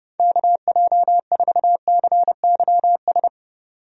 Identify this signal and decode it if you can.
Morse code, heard as a keyed tone, about 30 wpm, 700 Hz. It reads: KJ4CYS
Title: USA Call Signs - 6 Characters - 30wpm